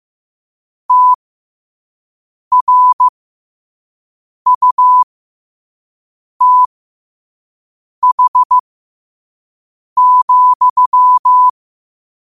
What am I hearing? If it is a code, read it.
Morse code, heard as a keyed tone, about 15 wpm, spaced as for 5 wpm, 1000 Hz. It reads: TRUTH,
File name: qotdfiles / 5wpm